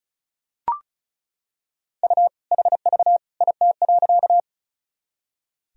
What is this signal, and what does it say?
Morse code, heard as a keyed tone, about 34 wpm, 700 Hz. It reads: U HV IT.
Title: Sentences from Top 100 Words and Abbreviations 35wpm